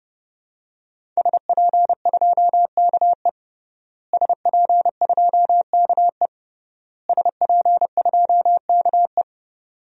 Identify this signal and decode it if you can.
Morse code, heard as a keyed tone, about 32 wpm, 700 Hz. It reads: SP2KE SP2KE SP2KE
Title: DXCC-100 Call Signs - 5 Characters - Repeated 3x - 30wpm